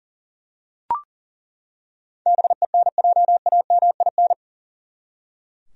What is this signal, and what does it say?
Morse code, heard as a keyed tone, about 40 wpm, 700 Hz. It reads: BENJAMIN
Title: USA Common Names - 40wpm